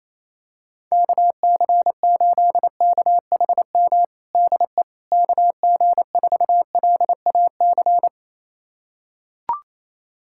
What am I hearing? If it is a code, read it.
Morse code, heard as a keyed tone, about 28 wpm, 700 Hz. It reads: KC8KHM DE KG4LAC
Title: USA Callsign DE Callsign - 6 Characters - 28wpm